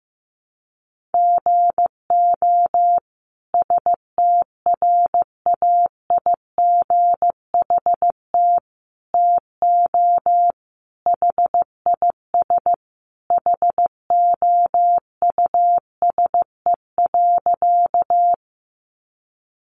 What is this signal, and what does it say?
Morse code, heard as a keyed tone, about 15 wpm, 700 Hz. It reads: GO STRAIGHT TO HIS HOUSE.
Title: Sentences from Top 1000 Words 15wpm